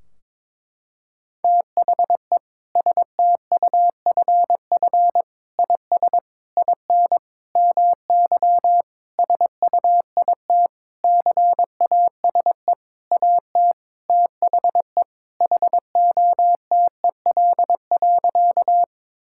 Morse code, heard as a keyed tone, about 22 wpm, 700 Hz. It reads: THE STUFF IS IN MY SUIT CASE AT THE HOTEL.
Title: Sentences from Top 1600 Words 22wpm